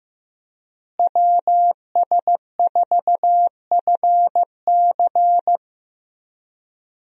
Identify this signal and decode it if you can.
Morse code, heard as a keyed tone, about 15 wpm, 700 Hz. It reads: WS4FC
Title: USA Call Signs - 5 Characters - Repeated 3x - 15wpm